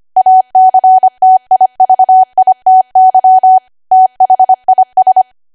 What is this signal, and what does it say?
Morse code, heard as a keyed tone, about 25 wpm, 750 Hz. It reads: ACTIVITY THIS